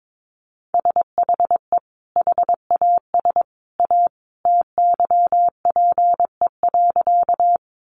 Morse code, heard as a keyed tone, about 22 wpm, 700 Hz. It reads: SHE HAS A TYPE.
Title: Sentences from Top 600 Words 22wpm